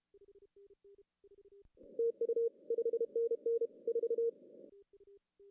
Morse code, heard as a keyed tone, about 30 wpm, 440 Hz. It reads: TU 5NN 4
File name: CW / cqww-cw-k5tr